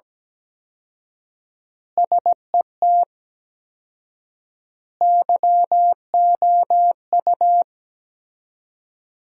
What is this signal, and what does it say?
Morse code, heard as a keyed tone, about 17 wpm, 700 Hz. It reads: SET YOU
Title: 4x Word-Spacing - Sentences from Top 200 Words - 17wpm